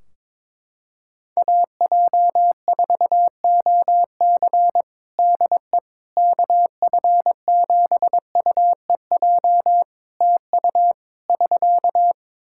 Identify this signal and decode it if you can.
Morse code, heard as a keyed tone, about 21 wpm, 700 Hz. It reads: AJ4OC DE KF7UEJ TU <SK>
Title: Practice QSO - Exchange 4 - Ending - 22wpm